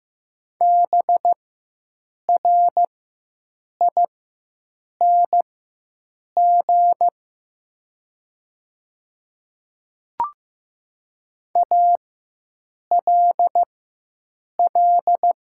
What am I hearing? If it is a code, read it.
Morse code, heard as a keyed tone, about 15 wpm, spaced as for 7 wpm, 690 Hz. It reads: BRING ALL
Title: 4x Character Spacing - Top 500 Words - 15wpm